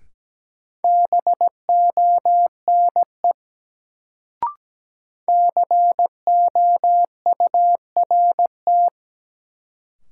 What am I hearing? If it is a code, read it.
Morse code, heard as a keyed tone, about 17 wpm, 700 Hz. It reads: BONE COURT
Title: Sentences from Top 1600 Words - New Word Drill - 17wpm